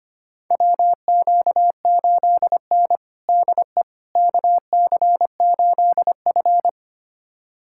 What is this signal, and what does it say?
Morse code, heard as a keyed tone, about 25 wpm, 700 Hz. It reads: WQ8N DE KC8F
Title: USA Callsign DE Callsign - 4 Characters - 25wpm